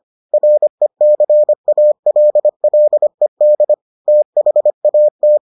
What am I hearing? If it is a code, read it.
Morse code, heard as a keyed tone, about 25 wpm, 600 Hz. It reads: RECALLED THAT